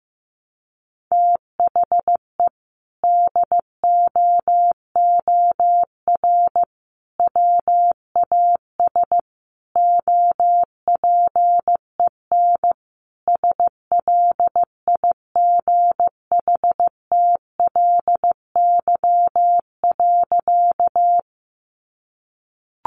Morse code, heard as a keyed tone, about 15 wpm, 700 Hz. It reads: THE DOOR WAS OPEN SLIGHTLY.
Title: Sentences from Top 1300 Words 15wpm